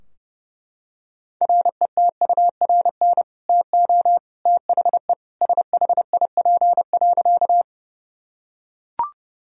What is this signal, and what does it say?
Morse code, heard as a keyed tone, about 30 wpm, 700 Hz. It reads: RETURN TO THE SHIP.
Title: Sentences from Top 1000 Words - Review Part 2 - 30wpm